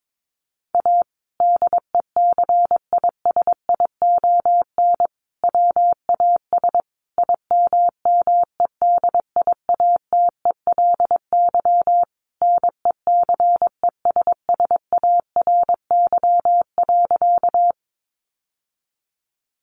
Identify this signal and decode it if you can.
Morse code, heard as a keyed tone, about 22 wpm, 700 Hz. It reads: A DECISION WAS IMMEDIATELY NECESSARY.
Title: Sentences from Top 1100 Words 22wpm